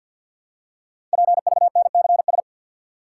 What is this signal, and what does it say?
Morse code, heard as a keyed tone, about 50 wpm, 700 Hz. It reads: PVNCS
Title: Code Groups - 5 Letter:Numbers - 50wpm